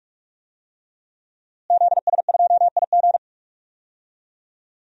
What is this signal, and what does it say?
Morse code, heard as a keyed tone, about 46 wpm, 700 Hz. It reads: ZS2IG
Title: DXCC-100 Call Signs - 5 Characters - Repeated 3x - 45wpm